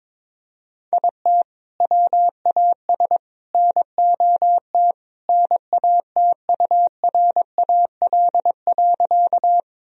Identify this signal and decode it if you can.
Morse code, heard as a keyed tone, about 22 wpm, 700 Hz. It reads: IT WAS NOT NATURAL.
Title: Sentences from Top 1000 Words - Review Part 4 - 22wpm